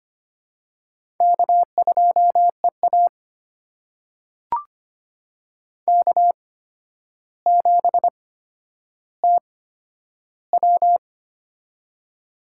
Morse code, heard as a keyed tone, about 25 wpm, spaced as for 11 wpm, 700 Hz. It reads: K2EA K7TW
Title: USA Call Signs - 4 Characters - 8x Character Spacing - 25wpm